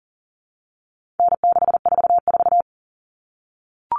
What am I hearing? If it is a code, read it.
Morse code, heard as a keyed tone, about 40 wpm, 700 Hz. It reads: N644
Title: All Cut Numbers - 4-Digit 40wpm